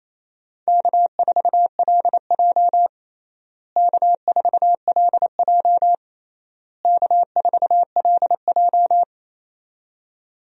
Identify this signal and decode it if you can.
Morse code, heard as a keyed tone, about 28 wpm, 700 Hz. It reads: K4LJ K4LJ K4LJ
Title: USA Call Signs - 4 Characters - Repeated 3x - 28wpm